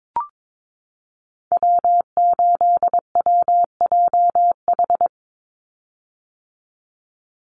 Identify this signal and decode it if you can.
Morse code, heard as a keyed tone, about 22 wpm, 700 Hz. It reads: W8WJH
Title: USA Call Signs 22wpm